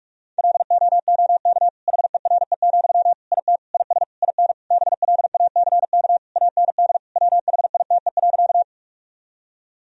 Morse code, heard as a keyed tone, about 42 wpm, 700 Hz. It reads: LOOK HERE, IT IS IN BLACK AND WHITE.
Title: Sentences from Top 1000 Words - Review Part 2 - 45wpm